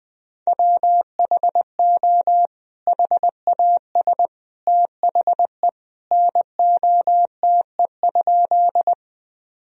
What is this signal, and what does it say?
Morse code, heard as a keyed tone, about 20 wpm, 700 Hz. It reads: WHO HAS THE NOTE?
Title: Sentences from Top 1000 Words - Review Part 2 - 20wpm